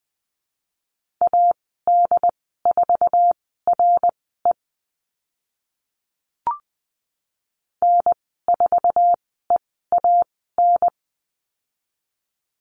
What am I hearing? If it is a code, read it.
Morse code, heard as a keyed tone, about 20 wpm, spaced as for 14 wpm, 700 Hz. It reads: AD4RE N4EAN
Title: 082 Lesson 14g - Number 2 - Call Signs 20wpm